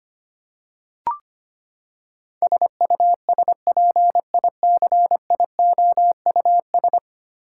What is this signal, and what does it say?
Morse code, heard as a keyed tone, about 25 wpm, 700 Hz. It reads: SUSPICIOUS
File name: Ten Letter Words - 25wpm